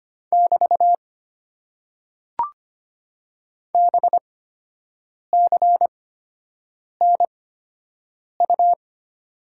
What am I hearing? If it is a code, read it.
Morse code, heard as a keyed tone, about 25 wpm, spaced as for 7 wpm, 690 Hz. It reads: <BT> BCNU
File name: Top 90 QSO Elements - 8x Character Spacing - 25wpm